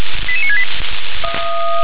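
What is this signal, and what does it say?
Five-tone selective call (ZVEI), heard as quick runs of five tones, tones of about 70 ms. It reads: 88968
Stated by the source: ZVEI five-tone call received on SDR.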